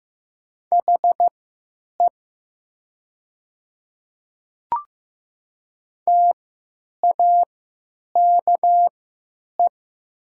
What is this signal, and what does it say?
Morse code, heard as a keyed tone, about 15 wpm, spaced as for 9 wpm, 700 Hz. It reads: HE TAKE
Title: 3x Character Spacing - Top 200 Words - 15wpm